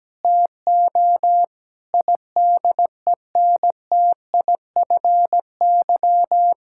Morse code, heard as a keyed tone, about 17 wpm, 700 Hz. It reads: TO IDENTIFY
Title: Sentences from Top 1000 Words - Review Part 6 - 17wpm